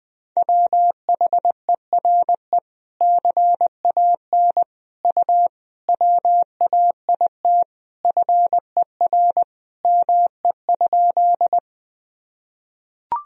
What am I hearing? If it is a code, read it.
Morse code, heard as a keyed tone, about 20 wpm, 700 Hz. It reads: WHERE CAN U WAIT FER ME?
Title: Sentences from Top 500 Words and Abbreviations 20wpm